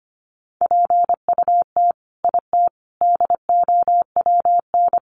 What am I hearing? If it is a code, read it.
Morse code, heard as a keyed tone, about 25 wpm, 700 Hz. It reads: PUT IT DOWN
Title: Sentences from Top 1400 Words 25wpm